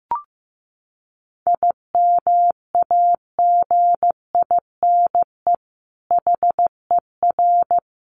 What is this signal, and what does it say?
Morse code, heard as a keyed tone, about 15 wpm, 700 Hz. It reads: IMAGINE HER
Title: Sentences from Top 1400 Words 15wpm